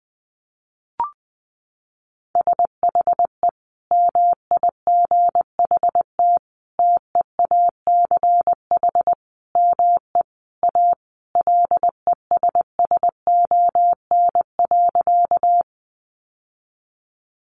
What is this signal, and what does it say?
Morse code, heard as a keyed tone, about 20 wpm, 700 Hz. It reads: SHE MIGHT TEACH ME A LESSON.
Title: Sentences from Top 1200 Words 20wpm